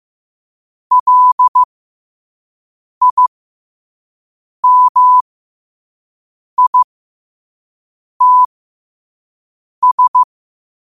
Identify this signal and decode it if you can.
Morse code, heard as a keyed tone, about 15 wpm, spaced as for 5 wpm, 1000 Hz. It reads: LIMITS
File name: qotdfiles / 5wpm